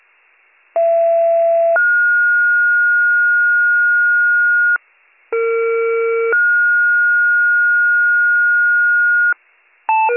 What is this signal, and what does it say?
Two-tone page, heard as a one-second tone, then a three-second tone, about 670 then 1430 Hz, about 460 then 1430 Hz. It is heard from 0.7 to 9.3 s.